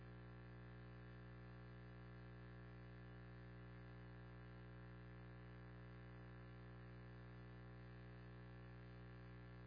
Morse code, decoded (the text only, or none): none